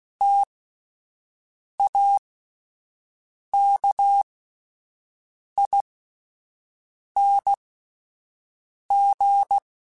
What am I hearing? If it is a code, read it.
Morse code, heard as a keyed tone, about 16 wpm, spaced as for 5 wpm, 800 Hz. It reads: TAKING